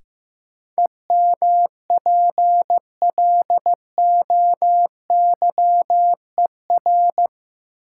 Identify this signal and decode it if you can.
Morse code, heard as a keyed tone, about 15 wpm, 700 Hz. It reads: EMPLOYER
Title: Sentences from Top 1600 Words - New Word Drill - 15wpm